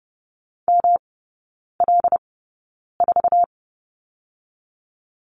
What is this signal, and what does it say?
Morse code, heard as a keyed tone, about 30 wpm, spaced as for 9 wpm, 700 Hz. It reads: ML4
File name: Lesson 18e - Letter m - Three Characters 30wpm